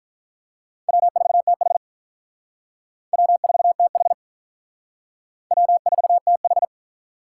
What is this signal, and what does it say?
Morse code, heard as a keyed tone, about 44 wpm, 700 Hz. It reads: W4TH W4TH W4TH